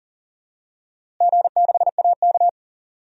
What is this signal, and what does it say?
Morse code, heard as a keyed tone, about 40 wpm, 700 Hz. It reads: G6AK
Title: DXCC-100 Call Signs - 4 Characters - Repeated 2x - 40wpm